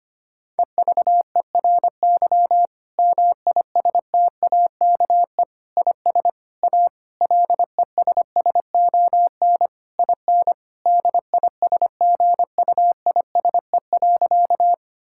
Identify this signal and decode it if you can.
Morse code, heard as a keyed tone, about 25 wpm, 700 Hz. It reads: EVERY MISTAKE IS A LESSON IN DISGUISE.
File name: practice / 25wpm / Difficult Sentences with Hints - 25wpm